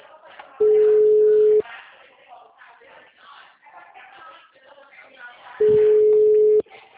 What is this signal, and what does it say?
Ringback tone (Europe, 425 Hz), with speech.